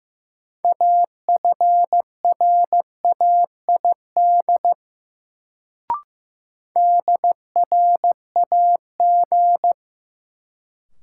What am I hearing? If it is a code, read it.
Morse code, heard as a keyed tone, about 15 wpm, 700 Hz. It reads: AFRAID DRAG
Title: Sentences from Top 1500 Words - New Word Drill Alt - 15wpm